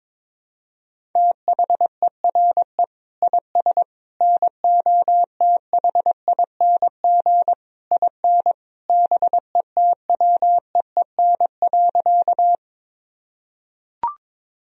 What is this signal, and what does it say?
Morse code, heard as a keyed tone, about 22 wpm, 700 Hz. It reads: THERE IS NOTHING IN BETWEEN.
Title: Sentences from Top 1000 Words - Review Part 1 - 22wpm